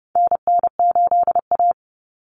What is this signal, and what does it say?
Morse code, heard as a keyed tone, about 30 wpm, 700 Hz. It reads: NN8A
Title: Common Cut Numbers - 4-Digit 30wpm